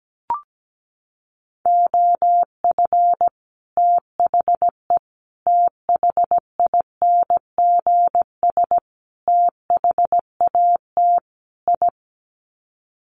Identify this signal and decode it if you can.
Morse code, heard as a keyed tone, about 17 wpm, 700 Hz. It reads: OF THE THINGS THAT I
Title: Sets of 5 Words Encore 17wpm